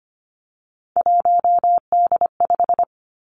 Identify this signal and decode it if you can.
Morse code, heard as a keyed tone, about 25 wpm, 700 Hz. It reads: 1D5
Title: Three Challenging Letter-Number Combinations 25wpm